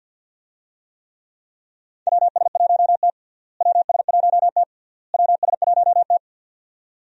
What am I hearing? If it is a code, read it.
Morse code, heard as a keyed tone, about 52 wpm, 700 Hz. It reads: WS1T WS1T WS1T
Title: USA Call Signs - 4 Characters - Repeated 3x - 50wpm